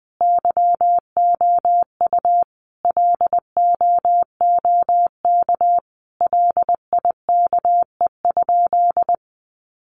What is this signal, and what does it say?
Morse code, heard as a keyed tone, about 20 wpm, 700 Hz. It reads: YOU LOOK LIKE?
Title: Sentences from Top 100 Words 20wpm